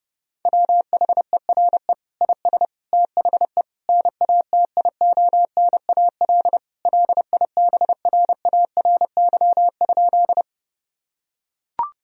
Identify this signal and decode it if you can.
Morse code, heard as a keyed tone, about 30 wpm, 700 Hz. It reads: WHERE IS THE NATIONAL LIBRARY?
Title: Sentences from Top 1000 Words - Review Part 4 - 30wpm